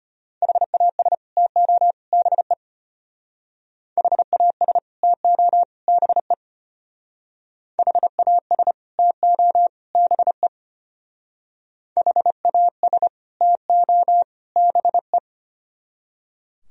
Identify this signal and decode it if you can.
Morse code, heard as a keyed tone, about 33 wpm, 700 Hz. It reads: HAS TO BE HAS TO BE HAS TO BE HAS TO BE